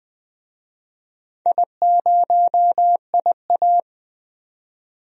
Morse code, heard as a keyed tone, about 20 wpm, 700 Hz. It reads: I0IA
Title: DXCC-100 Call Signs - 4 Characters - 20wpm